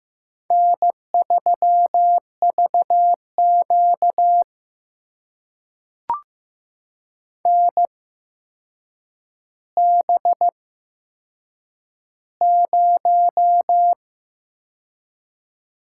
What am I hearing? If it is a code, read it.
Morse code, heard as a keyed tone, about 15 wpm, spaced as for 7 wpm, 700 Hz. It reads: N3VQ NB0
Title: USA Call Signs - 4 Characters - 8x Character Spacing - 15wpm